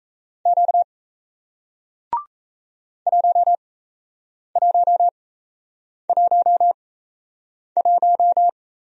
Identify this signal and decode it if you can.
Morse code, heard as a keyed tone, about 38 wpm, spaced as for 8 wpm, 700 Hz. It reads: Q 1111